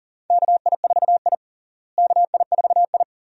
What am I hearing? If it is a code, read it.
Morse code, heard as a keyed tone, about 40 wpm, 700 Hz. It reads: KI4I KI4I
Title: USA Call Signs - 4 Characters - Repeated 2x - 40wpm